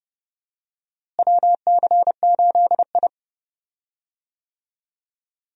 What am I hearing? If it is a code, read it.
Morse code, heard as a keyed tone, about 30 wpm, 700 Hz. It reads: WC8I
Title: USA Call Signs - 4 Characters - Repeated 2x - 30wpm